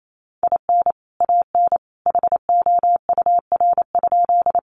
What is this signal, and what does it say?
Morse code, heard as a keyed tone, about 28 wpm, 700 Hz. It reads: IN AN HOUR?
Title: Sentences from Top 400 Words 28wpm